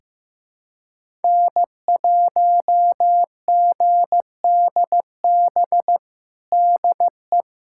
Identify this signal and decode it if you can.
Morse code, heard as a keyed tone, about 16 wpm, 700 Hz. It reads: N1GDB DE
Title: USA Callsign DE Callsign - 5 Characters - 15wpm